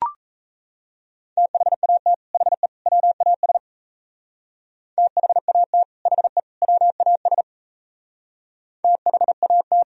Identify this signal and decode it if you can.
Morse code, heard as a keyed tone, about 33 wpm, 700 Hz. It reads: THAT HE WAS THAT HE WAS THAT
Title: Speed-Racing - Sets of 3 Words 28wpm